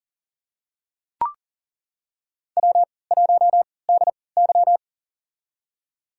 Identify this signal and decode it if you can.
Morse code, heard as a keyed tone, about 40 wpm, spaced as for 23 wpm, 700 Hz. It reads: W1DY
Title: USA Call Signs - 4 Characters - 3x Character Spacing - 40wpm